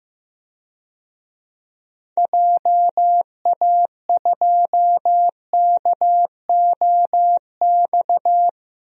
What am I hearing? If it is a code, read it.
Morse code, heard as a keyed tone, about 15 wpm, 700 Hz. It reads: JA2KOX
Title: DXCC-100 Call Signs - 6 Characters - 15wpm